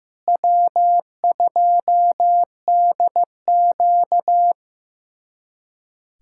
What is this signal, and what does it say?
Morse code, heard as a keyed tone, about 15 wpm, 700 Hz. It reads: W2DQ
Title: USA Callsign DE Callsign - 4 Characters - 15wpm